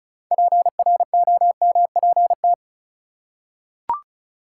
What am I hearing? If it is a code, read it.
Morse code, heard as a keyed tone, about 36 wpm, 700 Hz. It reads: PROMPT
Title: Six Letter Words - 35wpm